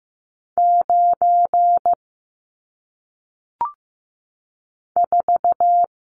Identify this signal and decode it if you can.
Morse code, heard as a keyed tone, about 15 wpm, 700 Hz. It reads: 9 4
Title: Numbers - 1-Digit 15wpm